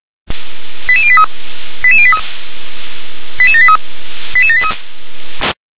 Five-tone selective call (ZVEI), heard as quick runs of five tones, tones of about 70 ms. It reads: 88083 88083 88783 88783